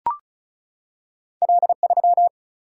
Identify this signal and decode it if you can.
Morse code, heard as a keyed tone, about 36 wpm, 700 Hz. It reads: L3
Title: Single-Letter Single-Number 35wpm